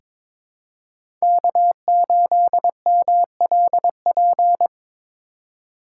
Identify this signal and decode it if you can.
Morse code, heard as a keyed tone, about 22 wpm, 700 Hz. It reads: K8MLP
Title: USA Call Signs - 5 Characters - 4x Character Spacing - 22wpm